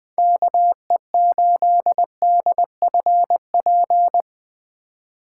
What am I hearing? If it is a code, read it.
Morse code, heard as a keyed tone, about 20 wpm, 700 Hz. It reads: KE8DFP
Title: USA Call Signs - 4x Character Spacing - 20wpm